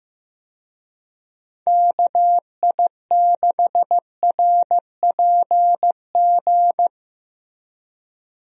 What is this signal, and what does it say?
Morse code, heard as a keyed tone, about 15 wpm, 700 Hz. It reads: KI6RPG
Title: USA Call Signs - 6 Characters - Repeated 2x - 15wpm